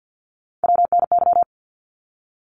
Morse code, heard as a keyed tone, about 48 wpm, 700 Hz. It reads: WNY
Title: ARRL Sections 50wpm